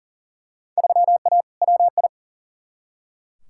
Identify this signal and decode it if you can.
Morse code, heard as a keyed tone, about 40 wpm, 700 Hz. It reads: 3A WI
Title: Field Day Exchanges - 40wpm